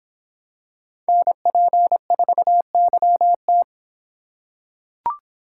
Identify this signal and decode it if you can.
Morse code, heard as a keyed tone, about 26 wpm, 700 Hz. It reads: NP4YT